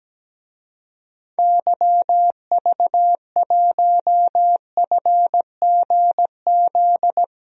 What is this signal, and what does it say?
Morse code, heard as a keyed tone, about 17 wpm, 700 Hz. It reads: YV1FGZ
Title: DXCC-100 Call Signs - 6 Characters - Repeated 3x - 17wpm